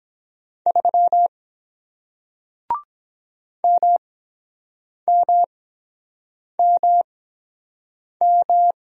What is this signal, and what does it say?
Morse code, heard as a keyed tone, about 23 wpm, spaced as for 7 wpm, 700 Hz. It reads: 3 MMMM